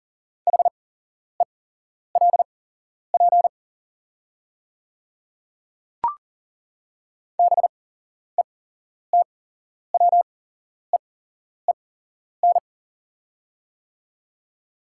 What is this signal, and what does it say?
Morse code, heard as a keyed tone, about 40 wpm, spaced as for 11 wpm, 700 Hz. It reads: HELP BETWEEN
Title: 8x Character Spacing - Top 300 Words - 40wpm